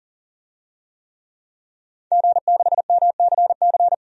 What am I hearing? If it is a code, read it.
Morse code, heard as a keyed tone, about 40 wpm, 700 Hz. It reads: G6MCC
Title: DXCC-100 Call Signs - 5 Characters - Repeated 2x - 40wpm